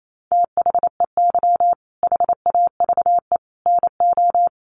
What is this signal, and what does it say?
Morse code, heard as a keyed tone, about 28 wpm, 700 Hz. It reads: THEY HAVE NO
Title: Sentences from Top 1300 Words 28wpm